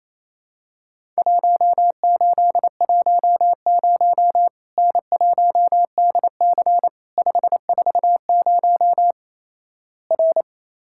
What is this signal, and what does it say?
Morse code, heard as a keyed tone, about 28 wpm, 700 Hz. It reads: 1810 N1DC 540 R
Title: Worked All Europe DX Contest - QTC Exchange - 28wpm